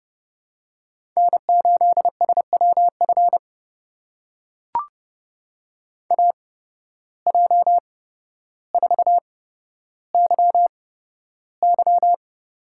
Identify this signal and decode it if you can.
Morse code, heard as a keyed tone, about 30 wpm, spaced as for 13 wpm, 700 Hz. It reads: N8SWF AJ4YY
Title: USA Call Signs - 5 Characters - 8x Character Spacing - 30wpm